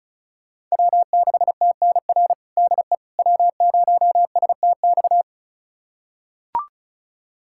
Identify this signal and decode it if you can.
Morse code, heard as a keyed tone, about 36 wpm, 700 Hz. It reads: W6TNR DE W0STX
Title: USA Callsign DE Callsign - 5 Characters - 35wpm